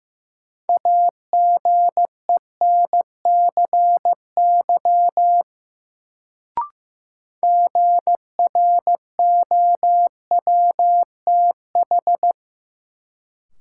Morse code, heard as a keyed tone, about 15 wpm, 700 Hz. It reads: AGENCY GROWTH
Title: Sentences from Top 1300 Words - New Word Drill Alt - 15wpm